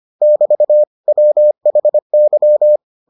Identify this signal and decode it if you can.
Morse code, heard as a keyed tone, about 25 wpm, 600 Hz. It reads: <BT> WHY